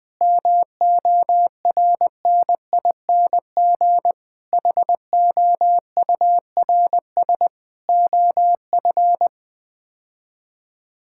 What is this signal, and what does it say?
Morse code, heard as a keyed tone, about 20 wpm, 700 Hz. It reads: MORNING HOURS OF
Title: Sets of 6 Words - 20wpm